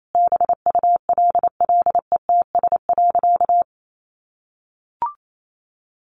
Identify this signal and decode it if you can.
Morse code, heard as a keyed tone, about 28 wpm, 700 Hz. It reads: BULLETS.
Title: A Sound Of Thunder 28wpm